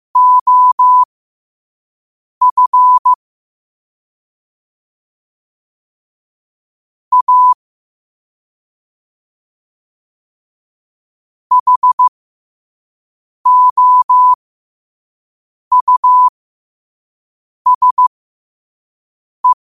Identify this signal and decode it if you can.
Morse code, heard as a keyed tone, about 15 wpm, spaced as for 5 wpm, 1000 Hz. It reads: OF A HOUSE
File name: qotdfiles / 5wpm